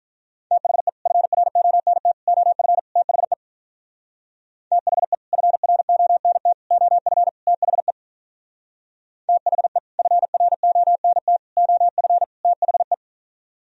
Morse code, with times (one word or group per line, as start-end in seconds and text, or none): 0.5-0.9 THE
1.1-2.1 FRONT
2.3-2.8 OF
3.0-3.3 THE
4.7-5.1 THE
5.3-6.5 FRONT
6.7-7.3 OF
7.5-7.9 THE
9.3-9.8 THE
10.0-11.4 FRONT
11.6-12.2 OF
12.4-12.9 THE